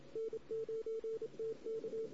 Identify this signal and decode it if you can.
Morse code, heard as a keyed tone, about 27 wpm, 440 Hz. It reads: N9TK
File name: CW / 2008-ss-cw-k5tr-op-n5rz -16:00